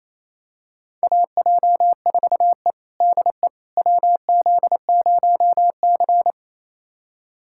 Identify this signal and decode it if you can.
Morse code, heard as a keyed tone, about 28 wpm, 700 Hz. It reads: AJ4E DE WZ0C
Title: USA Callsign DE Callsign - 4 Characters - 28wpm